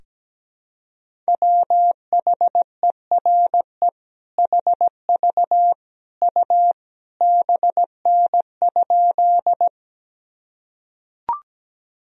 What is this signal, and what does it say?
Morse code, heard as a keyed tone, about 17 wpm, 700 Hz. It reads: WHERE HV U BN?
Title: Sentences from Top 200 Words and Abbreviations 17wpm